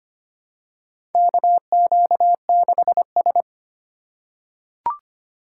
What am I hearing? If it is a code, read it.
Morse code, heard as a keyed tone, about 25 wpm, 700 Hz. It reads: KQ6S
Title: USA Call Signs - 4 Characters - 4x Character Spacing - 25wpm